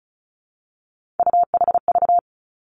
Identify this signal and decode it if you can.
Morse code, heard as a keyed tone, about 36 wpm, 700 Hz. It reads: UHV
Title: Three Challenging Letter-Number Combinations 35wpm